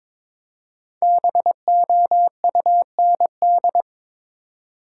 Morse code, heard as a keyed tone, about 22 wpm, 700 Hz. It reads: BOUND
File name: 5 Letter Words - First 3 Letters Given - 22wpm